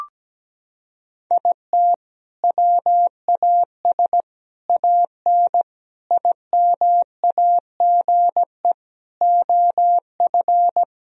Morse code, heard as a keyed tone, about 17 wpm, 700 Hz. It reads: IT WAS AN IMAGE OF
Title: Sentences from Top 1000 Words - Review Part 4 - 17wpm